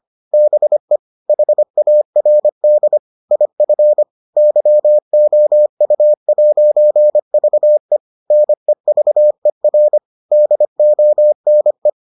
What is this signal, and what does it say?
Morse code, heard as a keyed tone, about 25 wpm, 600 Hz. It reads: BE HARD IF YOU'VE NEVER DONE